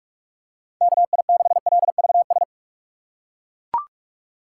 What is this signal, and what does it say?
Morse code, heard as a keyed tone, about 42 wpm, 700 Hz. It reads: KI6LVS
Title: USA Call Signs - 6x Character Spacing - 45wpm